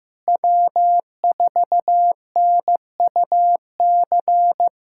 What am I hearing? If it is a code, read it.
Morse code, heard as a keyed tone, about 15 wpm, 700 Hz. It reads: W4NUC
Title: USA Callsign DE Callsign - 5 Characters - 15wpm